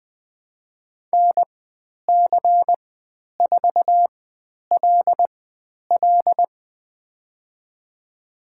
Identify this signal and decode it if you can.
Morse code, heard as a keyed tone, about 20 wpm, spaced as for 10 wpm, 700 Hz. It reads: NC4LL
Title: 095 Lesson 16g - Letter c - Call Signs 20wpm